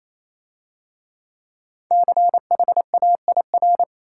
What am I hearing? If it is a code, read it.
Morse code, heard as a keyed tone, about 27 wpm, 700 Hz. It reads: CHAIR
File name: Easy Words - Speed-Racing with Hints - 22wpm